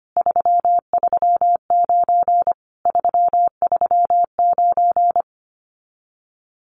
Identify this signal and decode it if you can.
Morse code, heard as a keyed tone, about 25 wpm, 700 Hz. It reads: 339 339
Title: Common QSO Phrases Encore 25wpm